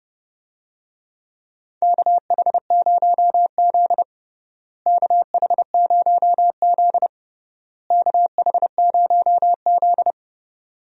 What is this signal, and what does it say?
Morse code, heard as a keyed tone, about 30 wpm, 700 Hz. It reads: KH0Z KH0Z KH0Z
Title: USA Call Signs - 4 Characters - Repeated 3x - 30wpm